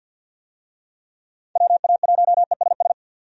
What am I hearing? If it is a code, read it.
Morse code, heard as a keyed tone, about 53 wpm, 700 Hz. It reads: WA1ESS
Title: USA Call Signs - 6 Characters - 50wpm